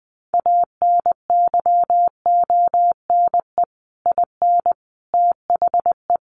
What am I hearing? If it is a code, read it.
Morse code, heard as a keyed tone, about 20 wpm, 700 Hz. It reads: ANYONE IN THE
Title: Sentences from Top 1200 Words 20wpm